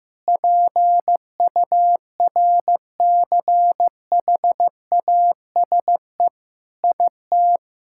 Morse code, heard as a keyed tone, about 15 wpm, 700 Hz. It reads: PURCHASE IT
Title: Sentences from Top 1000 Words - Review Part 5 - 15wpm